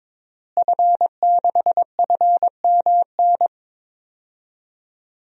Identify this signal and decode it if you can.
Morse code, heard as a keyed tone, about 22 wpm, 700 Hz. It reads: F6FMN
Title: DXCC-100 Call Signs - 5 Characters - Repeated 3x - 22wpm